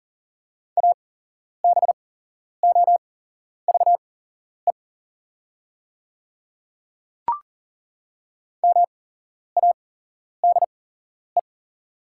Morse code, heard as a keyed tone, about 40 wpm, spaced as for 11 wpm, 690 Hz. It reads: ABOVE MADE
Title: 8x Character Spacing - Top 500 Words - 40wpm